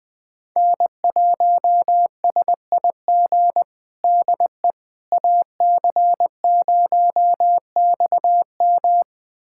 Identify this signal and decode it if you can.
Morse code, heard as a keyed tone, about 20 wpm, 700 Hz. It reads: N1SIG DE AC0XM
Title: USA Callsign DE Callsign - 5 Characters - 20wpm